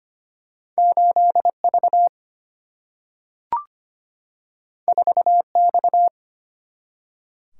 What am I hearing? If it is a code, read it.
Morse code, heard as a keyed tone, about 25 wpm, 700 Hz. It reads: 8V 4X
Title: Single-Letter Single-Number 25wpm